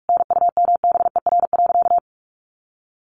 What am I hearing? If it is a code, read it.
Morse code, heard as a keyed tone, about 44 wpm, 700 Hz. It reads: NUMBER.
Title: Sentences from Top 1400 Words 45wpm